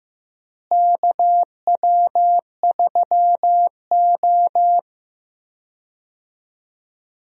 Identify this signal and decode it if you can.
Morse code, heard as a keyed tone, about 15 wpm, 700 Hz. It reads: KW3O
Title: USA Call Signs - 4 Characters - Repeated 3x - 15wpm